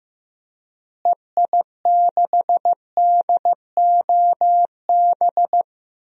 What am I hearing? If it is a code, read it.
Morse code, heard as a keyed tone, about 15 wpm, 700 Hz. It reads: EI6DOB
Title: DXCC-100 Call Signs - 6 Characters - Repeated 3x - 15wpm